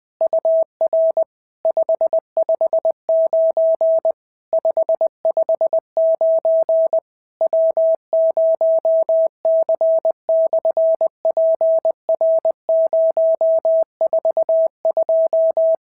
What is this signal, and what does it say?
Morse code, heard as a keyed tone, about 20 wpm, 650 Hz. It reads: UR 559 559 W0C/PR042
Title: SOTA QSO Practice 20wpm